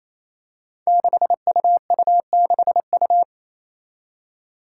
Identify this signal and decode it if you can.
Morse code, heard as a keyed tone, about 28 wpm, 700 Hz. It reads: 6UU6U
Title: Five Challenging Letter-Number Combinations 28wpm